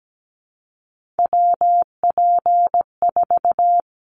Morse code, heard as a keyed tone, about 17 wpm, 700 Hz. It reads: WP4
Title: Callsign Prefixes for Top 25 DXCC - Repeated 3x - 17wpm